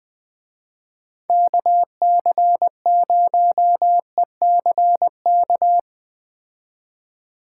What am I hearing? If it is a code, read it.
Morse code, heard as a keyed tone, about 19 wpm, 700 Hz. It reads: KC0ECK
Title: Speed Racing - USA 2x3 Call Signs 17wpm